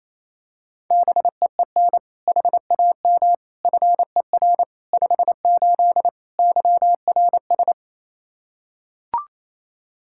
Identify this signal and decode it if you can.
Morse code, heard as a keyed tone, about 28 wpm, 700 Hz. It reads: BEEN HAM FER 58 YRS
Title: Practice QSO - Exchange 3 - More Info - 28wpm